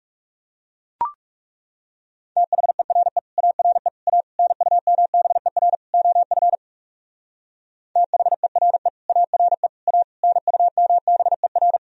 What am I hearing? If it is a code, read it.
Morse code, heard as a keyed tone, about 38 wpm, 700 Hz. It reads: THERE ARE A NUMBER OF THERE ARE A NUMBER